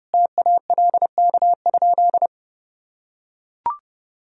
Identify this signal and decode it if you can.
Morse code, heard as a keyed tone, about 30 wpm, 700 Hz. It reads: TALK?
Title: Sentences from Top 1000 Words - Review Part 1 - 30wpm